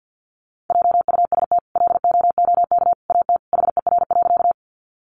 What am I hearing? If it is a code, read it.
Morse code, heard as a keyed tone, about 50 wpm, 700 Hz. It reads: JUST LOOK AT HER.
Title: Sentences from Top 200 Words 50wpm